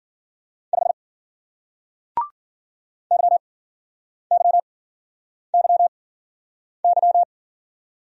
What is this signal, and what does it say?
Morse code, heard as a keyed tone, about 50 wpm, spaced as for 9 wpm, 700 Hz. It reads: 5 YYYY
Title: Speed-Racing Single Letter-Number 40wpm